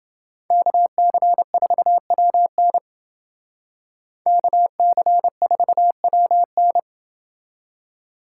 Morse code, heard as a keyed tone, about 29 wpm, 700 Hz. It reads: KC4WN KC4WN